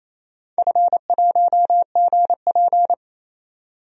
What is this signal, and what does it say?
Morse code, heard as a keyed tone, about 28 wpm, 700 Hz. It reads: F1GP
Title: DXCC-100 Call Signs - 4 Characters - 28wpm